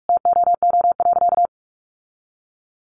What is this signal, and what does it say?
Morse code, heard as a keyed tone, about 44 wpm, 700 Hz. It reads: TOO.
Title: Sentences from Top 1500 Words 45wpm